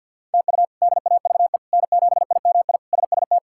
Morse code, heard as a keyed tone, about 49 wpm, 700 Hz. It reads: TU DAVE N7IMI SST
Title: K1USN SST QSO Practice - 50wpm